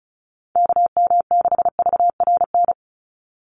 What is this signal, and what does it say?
Morse code, heard as a keyed tone, about 34 wpm, 700 Hz. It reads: KM6VRN
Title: USA Call Signs 35wpm